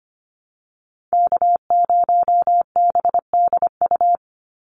Morse code, heard as a keyed tone, about 25 wpm, 700 Hz. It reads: K0BDU
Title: USA Call Signs 25wpm